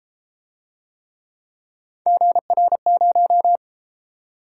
Morse code, heard as a keyed tone, about 33 wpm, 700 Hz. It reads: GR0